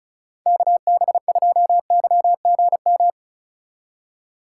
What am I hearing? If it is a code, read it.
Morse code, heard as a keyed tone, about 36 wpm, 700 Hz. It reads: KB2YGM
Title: USA Call Signs - 6 Characters - 2x Character Spacing - 35wpm